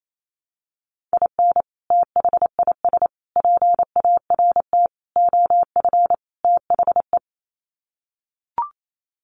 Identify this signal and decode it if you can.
Morse code, heard as a keyed tone, about 28 wpm, 700 Hz. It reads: IN THIS PART OF THE
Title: Sets of 5 Words 28wpm